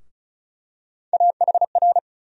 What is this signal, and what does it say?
Morse code, heard as a keyed tone, about 36 wpm, 700 Hz. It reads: AHR
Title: USA Call Sign Postfixes - 3 Letters 35wpm